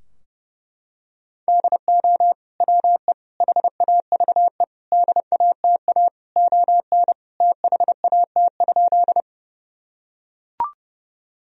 Morse code, heard as a keyed tone, about 32 wpm, 700 Hz. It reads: DO WE HAVE DATA ON THAT?
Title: Sentences from Top 1000 Words - Review Part 3 - 30wpm